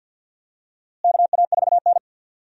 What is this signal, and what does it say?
Morse code, heard as a keyed tone, about 51 wpm, 700 Hz. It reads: KA4N
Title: USA Call Signs - 4 Characters - Repeated 2x - 50wpm